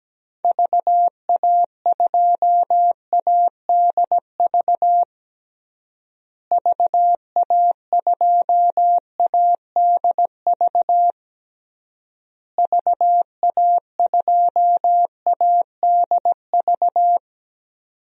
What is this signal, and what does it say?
Morse code, heard as a keyed tone, about 17 wpm, 700 Hz. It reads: VA2ADV VA2ADV VA2ADV
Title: DXCC-100 Call Signs - 6 Characters - Repeated 3x - 17wpm